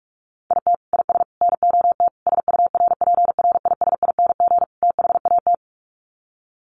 Morse code, heard as a keyed tone, about 44 wpm, 700 Hz. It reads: IT IS NOT SURPRISING THAT
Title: Sets of 5 Words 45wpm